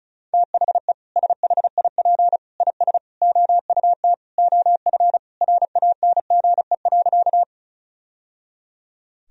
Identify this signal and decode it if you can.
Morse code, heard as a keyed tone, about 36 wpm, 700 Hz. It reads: THE SHIP IS OUT OF RANGE.
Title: Sentences from Top 1000 Words - Review Part 4 - 35wpm